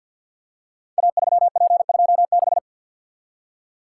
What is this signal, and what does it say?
Morse code, heard as a keyed tone, about 48 wpm, 700 Hz. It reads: A3P26
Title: Code Groups - 5 Letter:Numbers - 50wpm